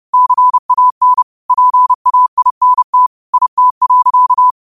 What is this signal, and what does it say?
Morse code, heard as a keyed tone, about 30 wpm, 1000 Hz. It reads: CAN PAINT IT.
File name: 30wpm